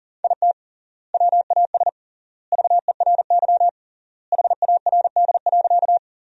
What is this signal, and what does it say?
Morse code, heard as a keyed tone, about 37 wpm, 700 Hz. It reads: IT WAS VERY HARD.
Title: 3x Word-Spacing - Sentences from Top 300 Words - 40wpm